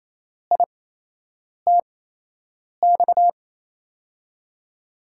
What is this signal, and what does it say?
Morse code, heard as a keyed tone, about 28 wpm, spaced as for 8 wpm, 700 Hz. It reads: ITX
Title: USA Call Signs - 6 Characters - 8x Character Spacing - 28wpm